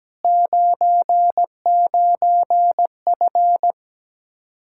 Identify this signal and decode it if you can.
Morse code, heard as a keyed tone, about 17 wpm, 700 Hz. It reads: 99F
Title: Speed-Racing Common QSO Phrases - 17wpm